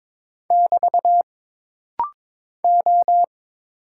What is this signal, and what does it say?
Morse code, heard as a keyed tone, about 22 wpm, 700 Hz. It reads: <BT> O
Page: This rapid-fire timing is geared to push your ICR (Instant Character Recognition) skills.